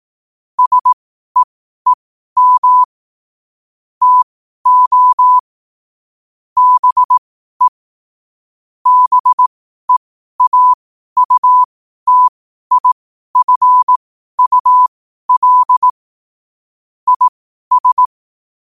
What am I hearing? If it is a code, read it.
Morse code, heard as a keyed tone, about 18 wpm, spaced as for 12 wpm, 1000 Hz. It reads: SEEM TO BE BEAUTIFUL IS
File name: qotdfiles / 12wpm